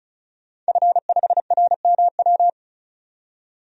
Morse code, heard as a keyed tone, about 36 wpm, 700 Hz. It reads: F5RMW
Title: DXCC-100 Call Signs - 5 Characters - 35wpm